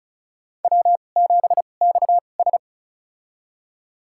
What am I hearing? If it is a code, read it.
Morse code, heard as a keyed tone, about 34 wpm, spaced as for 25 wpm, 700 Hz. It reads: W7XS
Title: USA Call Signs - 4 Characters - 2x Character Spacing - 35wpm